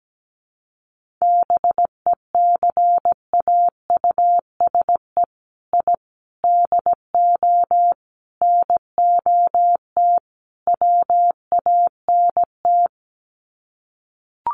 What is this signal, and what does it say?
Morse code, heard as a keyed tone, about 17 wpm, 700 Hz. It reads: BECAUSE I DO NOT WANT
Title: Sets of 5 Words Encore 17wpm